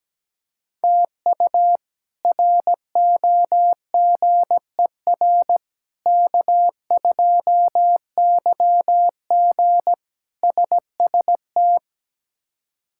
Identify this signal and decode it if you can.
Morse code, heard as a keyed tone, about 17 wpm, 700 Hz. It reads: TU ROGER K2YG SST
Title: K1USN SST QSO Practice - 17wpm